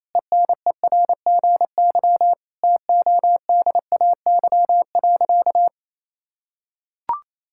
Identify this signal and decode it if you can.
Morse code, heard as a keyed tone, about 28 wpm, 700 Hz. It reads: ENERGY TODAY.
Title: Sentences from Top 1000 Words - Review Part 6 - 28wpm